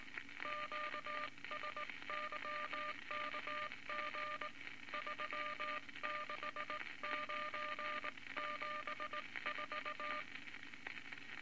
Morse code, heard as a keyed tone, about 20 wpm, 1200 Hz, with noise. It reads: QSYKG 36974
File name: Morse Code Records / 19 B LCM 167 Twelve Words Per Minute